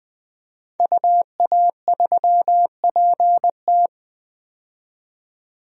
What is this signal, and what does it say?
Morse code, heard as a keyed tone, about 21 wpm, 700 Hz. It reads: UA3PT
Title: DXCC-100 Call Signs - 5 Characters - 20wpm